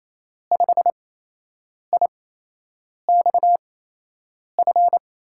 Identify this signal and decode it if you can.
Morse code, heard as a keyed tone, about 28 wpm, spaced as for 8 wpm, 700 Hz. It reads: 5IXF
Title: USA Call Signs - 6 Characters - 8x Character Spacing - 28wpm